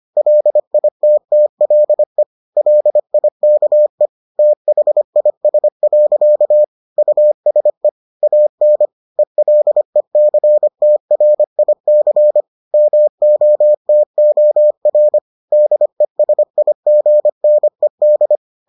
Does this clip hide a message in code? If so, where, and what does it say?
Yes: Morse code from 0.2 to 18.4 s, heard as a keyed tone, about 25 wpm, 600 Hz. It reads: LITTLE LIKE THIS. USE AN ELECTRIC MOTOR DESIGNED